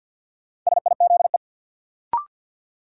Morse code, heard as a keyed tone, about 53 wpm, 700 Hz. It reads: SIZE
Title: Sets of 3 Words - New Word Drill - 50wpm